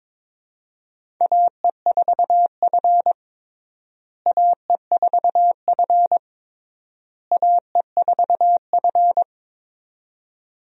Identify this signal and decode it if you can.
Morse code, heard as a keyed tone, about 22 wpm, 700 Hz. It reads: AE4F AE4F AE4F
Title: USA Call Signs - 4 Characters - Repeated 3x - 22wpm